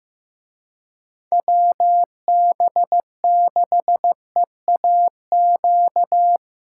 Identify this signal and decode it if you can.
Morse code, heard as a keyed tone, about 17 wpm, 700 Hz. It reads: WB6EAQ
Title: Speed Racing - USA 2x3 Call Signs 15wpm